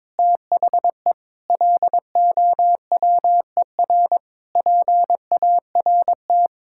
Morse code, heard as a keyed tone, about 22 wpm, 700 Hz. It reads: THE LOWER PART
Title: Sentences from Top 1000 Words - Review Part 5 - 22wpm